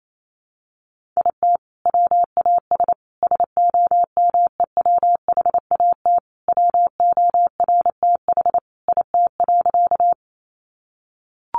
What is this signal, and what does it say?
Morse code, heard as a keyed tone, about 28 wpm, 700 Hz. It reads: IT WAS SOMEWHAT WORTH IT.
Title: Sentences from Top 1300 Words 28wpm